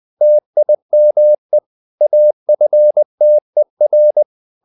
Morse code, heard as a keyed tone, about 20 wpm, 600 Hz. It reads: TIME AFTER
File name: the old man and the sea part 2 20wpm